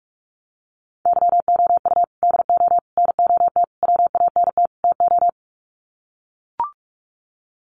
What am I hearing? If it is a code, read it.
Morse code, heard as a keyed tone, about 44 wpm, 700 Hz. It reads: YOU DO NOT WANT TO
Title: Sets of 5 Words 45wpm